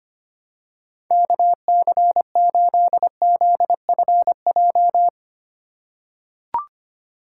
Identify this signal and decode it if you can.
Morse code, heard as a keyed tone, about 25 wpm, 700 Hz. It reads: KC8ZFJ
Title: USA Call Signs - 6 Characters - 4x Character Spacing - 25wpm